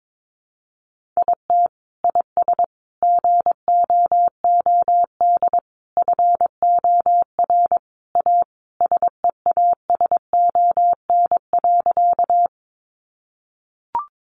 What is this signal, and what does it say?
Morse code, heard as a keyed tone, about 22 wpm, 700 Hz. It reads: IT IS GOOD FOR A SEASON.
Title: Sentences from Top 1400 Words 22wpm